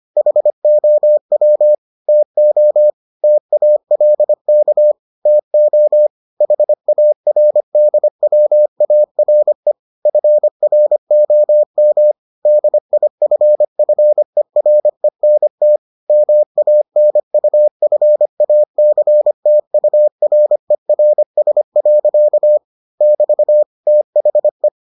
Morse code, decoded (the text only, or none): HOW TO TALK TO HARDWARE FROM DIFFERENT MANUFACTURERS. <BT> THE